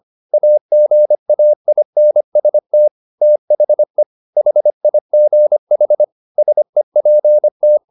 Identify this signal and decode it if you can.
Morse code, heard as a keyed tone, about 25 wpm, 600 Hz. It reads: AGAINST THE HIGH SEPT
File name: the old man and the sea part 1 25wpm